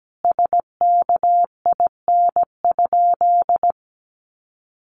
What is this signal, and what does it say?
Morse code, heard as a keyed tone, about 17 wpm, 700 Hz. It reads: SKIN?
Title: Sentences from Top 1300 Words 17wpm